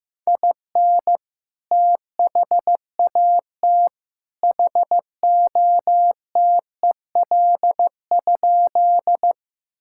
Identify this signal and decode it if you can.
Morse code, heard as a keyed tone, about 15 wpm, 700 Hz. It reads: IN THAT HOTEL?
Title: Sentences from Top 1600 Words 15wpm